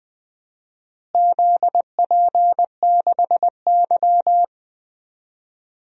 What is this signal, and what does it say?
Morse code, heard as a keyed tone, about 20 wpm, 700 Hz. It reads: ZP6Y
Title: DXCC-100 Call Signs - 4 Characters - Repeated 2x - 20wpm